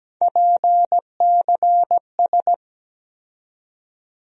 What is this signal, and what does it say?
Morse code, heard as a keyed tone, about 17 wpm, 700 Hz. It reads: PCS
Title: USA Call Sign Postfixes - 3 Letters 17wpm